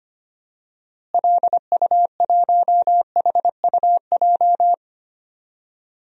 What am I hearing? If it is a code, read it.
Morse code, heard as a keyed tone, about 25 wpm, 700 Hz. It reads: LU1HUJ
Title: DXCC-100 Call Signs - 6 Characters - Repeated 3x - 25wpm